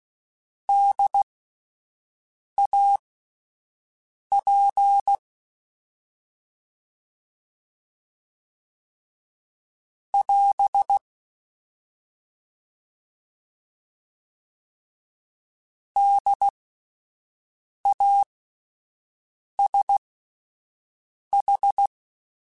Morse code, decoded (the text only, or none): DAP <AS> DASH